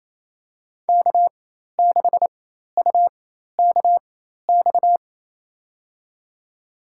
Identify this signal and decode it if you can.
Morse code, heard as a keyed tone, about 28 wpm, spaced as for 13 wpm, 700 Hz. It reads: K6UKX
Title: USA Call Signs - 5 Characters - 4x Character Spacing - 28wpm